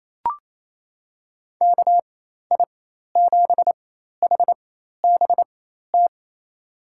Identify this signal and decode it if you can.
Morse code, heard as a keyed tone, about 28 wpm, spaced as for 13 wpm, 700 Hz. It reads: KI7HBT
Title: USA Call Signs - 6 Characters - 4x Character Spacing - 28wpm